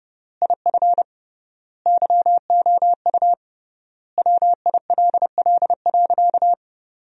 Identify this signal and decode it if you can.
Morse code, heard as a keyed tone, about 32 wpm, 700 Hz. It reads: IF YOU WILL.
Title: 3x Word-Spacing - Sentences from Top 500 Words - 30wpm